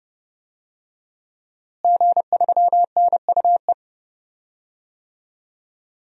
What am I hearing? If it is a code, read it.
Morse code, heard as a keyed tone, about 30 wpm, 700 Hz. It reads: G3NUE
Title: DXCC-100 Call Signs - 5 Characters - Repeated 3x - 30wpm